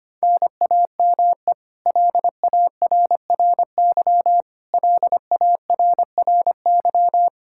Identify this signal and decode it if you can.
Morse code, heard as a keyed tone, about 25 wpm, 700 Hz. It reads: NAME LARRY LARRY
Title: Practice QSO - Exchange 1 - Beginning - 25wpm